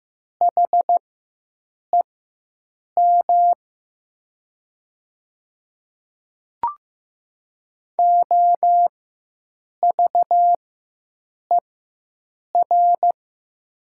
Morse code, heard as a keyed tone, about 15 wpm, spaced as for 7 wpm, 700 Hz. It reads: HEM OVER
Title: 4x Character Spacing - Top 100 Words - 15wpm